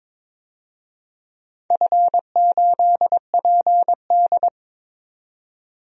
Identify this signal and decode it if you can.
Morse code, heard as a keyed tone, about 23 wpm, 700 Hz. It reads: F8PD
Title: DXCC-100 Call Signs - 4 Characters - 22wpm